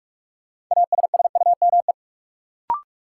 Morse code, heard as a keyed tone, about 45 wpm, 700 Hz. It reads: ASSUME
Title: Sentences from Top 1500 Words - New Word Drill - 45wpm